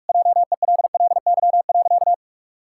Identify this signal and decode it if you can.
Morse code, heard as a keyed tone, about 46 wpm, 700 Hz. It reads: JELLY.
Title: Pangram Challenge v2 - 45wpm